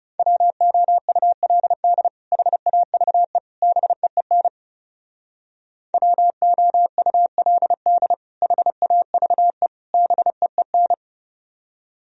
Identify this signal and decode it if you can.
Morse code, heard as a keyed tone, about 38 wpm, 700 Hz. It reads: WOULD HAVE BEEN WOULD HAVE BEEN